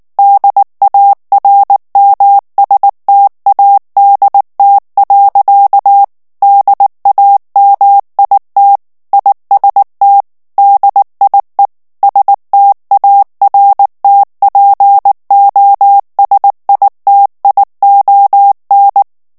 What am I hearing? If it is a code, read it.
Morse code, heard as a keyed tone, about 19 wpm, 800 Hz. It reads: DARMSTADT. DAMIT IST DIE STARTPOSITION
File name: morse mp3 / ALLEMAND texte clair 2 vitesse 18 mots minute